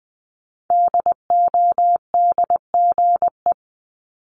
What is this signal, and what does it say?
Morse code, heard as a keyed tone, about 20 wpm, 700 Hz. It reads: DODGE
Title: Multi-letter Phonemes Repeated 3x 20wpm